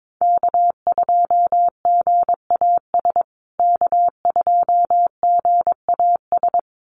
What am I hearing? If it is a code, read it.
Morse code, heard as a keyed tone, about 22 wpm, 700 Hz. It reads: K2GAS K2GAS
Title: USA Call Signs Repeated 2x 22wpm